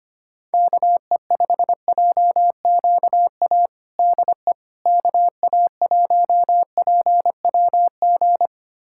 Morse code, heard as a keyed tone, about 25 wpm, 700 Hz. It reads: KE5JQA DE KA1PWG
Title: USA Callsign DE Callsign 25wpm